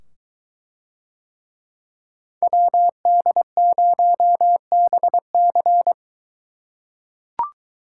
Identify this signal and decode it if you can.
Morse code, heard as a keyed tone, about 21 wpm, 700 Hz. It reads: WD0BC